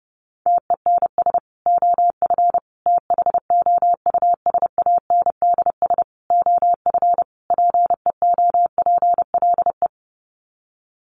Morse code, heard as a keyed tone, about 30 wpm, 700 Hz. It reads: TENS OF THOUSANDS OF PEOPLE
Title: Sets of 5 Words Encore2 30wpm